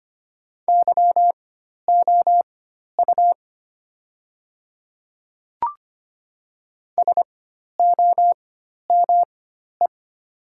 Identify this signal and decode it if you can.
Morse code, heard as a keyed tone, about 25 wpm, spaced as for 12 wpm, 700 Hz. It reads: YOU SOME
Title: 4x Character Spacing - Top 100 Words - 25wpm